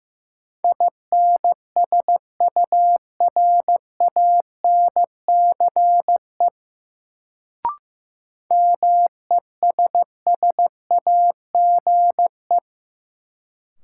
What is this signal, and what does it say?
Morse code, heard as a keyed tone, about 15 wpm, 700 Hz. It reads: INSURANCE MESSAGE
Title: Sentences from Top 1100 Words - New Word Drill Alt - 15wpm